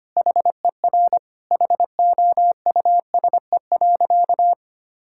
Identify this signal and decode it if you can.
Morse code, heard as a keyed tone, about 25 wpm, 700 Hz. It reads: HER HOUSE.
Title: Sentences from Top 1000 Words - Review Part 6 - 25wpm